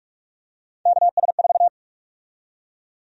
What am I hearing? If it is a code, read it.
Morse code, heard as a keyed tone, about 46 wpm, 700 Hz. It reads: KS4
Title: USA Call Sign Prefixes - Repeated 2x - 45wpm